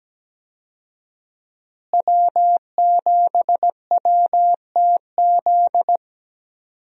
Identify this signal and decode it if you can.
Morse code, heard as a keyed tone, about 17 wpm, 700 Hz. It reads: W7WTZ
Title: USA Call Signs - 5 Characters - Repeated 3x - 17wpm